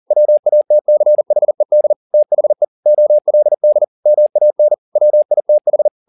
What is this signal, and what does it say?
Morse code, heard as a keyed tone, about 40 wpm, 600 Hz. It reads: WATCHED THE OLD MAN WITH
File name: the old man and the sea 2 40wpm